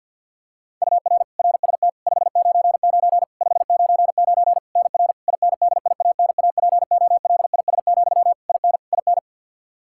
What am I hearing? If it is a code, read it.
Morse code, heard as a keyed tone, about 49 wpm, 700 Hz. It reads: UR RST 599 599 NR INDIANAPOLIS, IN IN
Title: Practice QSO - Exchange 1 - Beginning - 50wpm